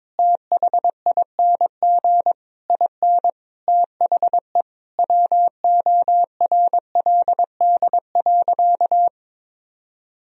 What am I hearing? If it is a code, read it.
Morse code, heard as a keyed tone, about 22 wpm, 700 Hz. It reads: THING IN THE WORLD.
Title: Sentences from Top 1000 Words - Review Part 1 - 22wpm